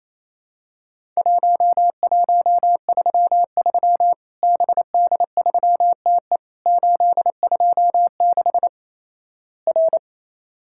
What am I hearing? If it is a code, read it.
Morse code, heard as a keyed tone, about 28 wpm, 700 Hz. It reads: 1133 BD3TE 826 R
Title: Worked All Europe DX Contest - QTC Exchange - 28wpm